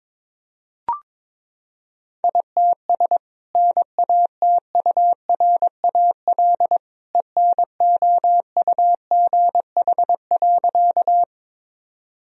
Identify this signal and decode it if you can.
Morse code, heard as a keyed tone, about 22 wpm, 700 Hz. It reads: ITS NATURAL ENOUGH.
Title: Sentences from Top 1000 Words - Review Part 4 - 22wpm